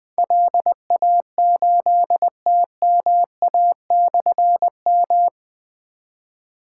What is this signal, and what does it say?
Morse code, heard as a keyed tone, about 20 wpm, 700 Hz. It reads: LA8TMA/M
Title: Difficult Call Signs 20wpm